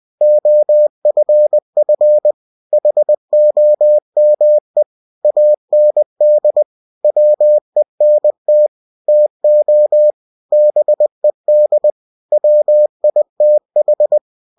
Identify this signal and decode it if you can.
Morse code, heard as a keyed tone, about 20 wpm, 600 Hz. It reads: OFF HOME AND WENT TO BED WITH
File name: cw-books / winnie-the-pooh / winnie-20wpm-page-0043